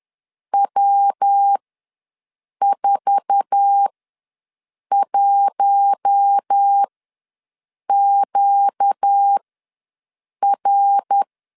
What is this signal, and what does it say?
Morse code, heard as a keyed tone, about 10 wpm, spaced as for 6 wpm, 800 Hz. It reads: W41QR
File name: LEZIONI / 8 YJ45 / 8 YJ4 ws 10wpm 4020